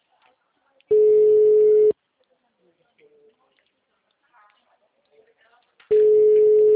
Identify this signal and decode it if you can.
Ringback tone (Europe, 425 Hz), with speech.
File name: out- -2008- - - .86343